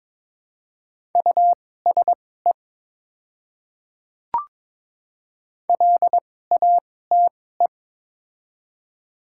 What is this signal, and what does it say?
Morse code, heard as a keyed tone, about 21 wpm, spaced as for 16 wpm, 700 Hz. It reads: USE LATE
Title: 2x Character Spacing - Top 300 Words - 22wpm